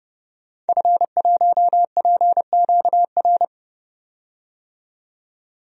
Morse code, heard as a keyed tone, about 30 wpm, 700 Hz. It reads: F1PQR
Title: DXCC-100 Call Signs - 5 Characters - 30wpm